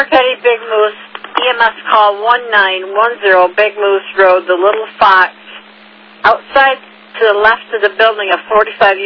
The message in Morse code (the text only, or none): none